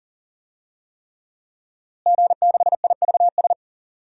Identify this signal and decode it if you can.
Morse code, heard as a keyed tone, about 40 wpm, 700 Hz. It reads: G6IVS
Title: DXCC-100 Call Signs - 5 Characters - Repeated 3x - 40wpm